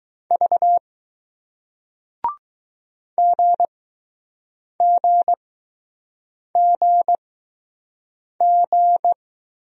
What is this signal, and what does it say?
Morse code, heard as a keyed tone, about 20 wpm, spaced as for 6 wpm, 700 Hz. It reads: V GGGG